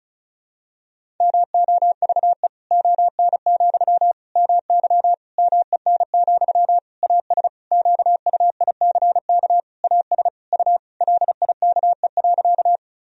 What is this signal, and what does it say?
Morse code, heard as a keyed tone, about 34 wpm, 700 Hz. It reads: MOVE ON, MY MEN, AS QUICK AS U LIKE.
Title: Sentences from Top 500 Words and Abbreviations 35wpm